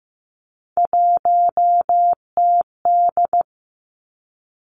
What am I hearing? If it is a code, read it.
Morse code, heard as a keyed tone, about 15 wpm, 700 Hz. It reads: 1TD
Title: Three Challenging Letter-Number Combinations 15wpm